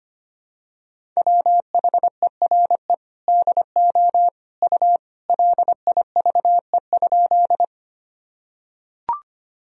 Morse code, heard as a keyed tone, about 25 wpm, 700 Hz. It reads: WHERE DO U LIVE?
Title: Sentences from Top 200 Words and Abbreviations 25wpm